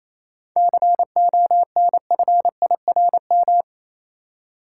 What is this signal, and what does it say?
Morse code, heard as a keyed tone, about 28 wpm, 700 Hz. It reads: CONFIRM
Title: Seven Letter Words - 28wpm